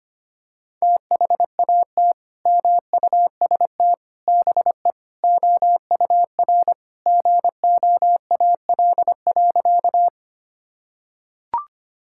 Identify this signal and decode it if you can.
Morse code, heard as a keyed tone, about 25 wpm, 700 Hz. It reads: THAT MUST BE OUR GOAL.
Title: Sentences from Top 1000 Words - Review Part 5 - 25wpm